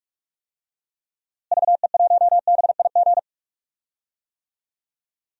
Morse code, heard as a keyed tone, about 46 wpm, 700 Hz. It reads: VE1BIG
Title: DXCC-100 Call Signs - 6 Characters - Repeated 3x - 45wpm